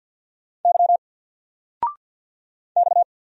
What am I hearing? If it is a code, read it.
Morse code, heard as a keyed tone, about 49 wpm, 700 Hz. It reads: Y X
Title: Single Character (Letter:Number:Punctuation:Prosign) 50wpm